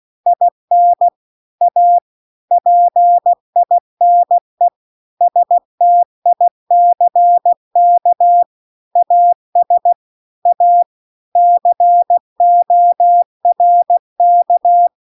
Morse code, heard as a keyed tone, about 16 wpm, 700 Hz. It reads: IN A PINE STICK AS A CORK
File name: CW-The-Big-Twoheaded-River-2-16WPM